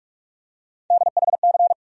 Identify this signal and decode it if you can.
Morse code, heard as a keyed tone, about 46 wpm, 700 Hz. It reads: DHC
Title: USA Call Sign Postfixes - 3 Letters 45wpm